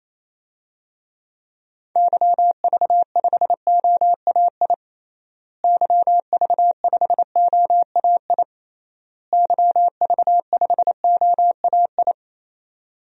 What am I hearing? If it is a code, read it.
Morse code, heard as a keyed tone, about 27 wpm, 700 Hz. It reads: YV5OAI YV5OAI YV5OAI
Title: DXCC-100 Call Signs - 6 Characters - Repeated 3x - 28wpm